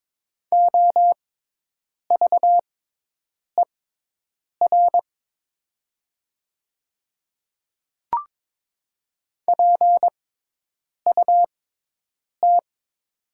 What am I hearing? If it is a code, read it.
Morse code, heard as a keyed tone, about 22 wpm, spaced as for 8 wpm, 700 Hz. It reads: OVER PUT
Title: 6x Character Spacing - Top 200 Words - 22wpm